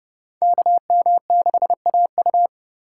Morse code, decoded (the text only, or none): KM6AU